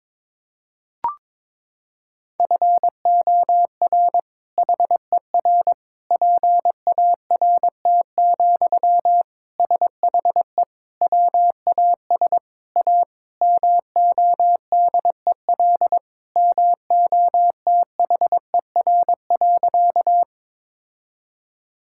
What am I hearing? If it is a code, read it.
Morse code, heard as a keyed tone, about 22 wpm, 700 Hz. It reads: FOR HER PART, SHE WAS A MODEL MOTHER.
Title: Sentences from Top 1000 Words - Review Part 5 - 22wpm